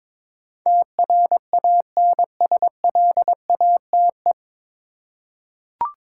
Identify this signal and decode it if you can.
Morse code, heard as a keyed tone, about 22 wpm, 700 Hz. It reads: TRANSLATE
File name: 9 Letter Words - First 3 Letters Given - 22wpm